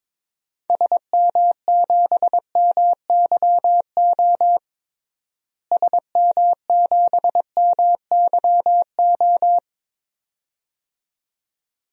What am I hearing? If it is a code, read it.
Morse code, heard as a keyed tone, about 22 wpm, 700 Hz. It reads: SM7MYO SM7MYO
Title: DXCC-100 Call Signs - 6 Characters - Repeated 2x - 22wpm